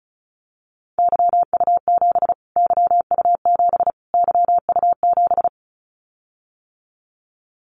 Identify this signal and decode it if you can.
Morse code, heard as a keyed tone, about 34 wpm, 700 Hz. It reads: YU7 YU7 YU7
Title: Callsign Prefixes for Top 25 DXCC - Repeated 3x - 35wpm